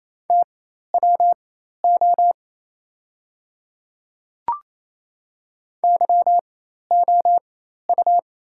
Morse code, heard as a keyed tone, about 28 wpm, spaced as for 13 wpm, 690 Hz. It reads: TWO YOU
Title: 4x Character Spacing - Top 100 Words - 28wpm